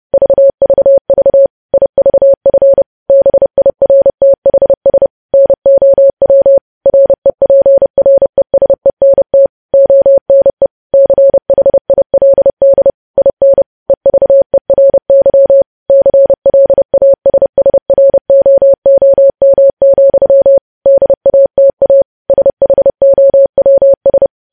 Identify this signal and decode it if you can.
Morse code, heard as a keyed tone, about 30 wpm, 550 Hz. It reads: VVV IVF BIRTHS NOW REPRESENT ONE CHILD IN EVERY CLASSROOM, DATA SHOWS
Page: News Headlines in Morse Code at 30 WPM